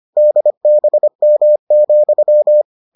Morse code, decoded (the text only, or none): DBM,